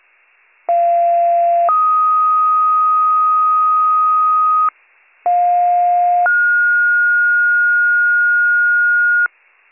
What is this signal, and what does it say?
Two-tone page, heard as a one-second tone, then a three-second tone, about 690 then 1250 Hz, about 710 then 1430 Hz, with noise. It is heard from 0.7 to 9.3 s.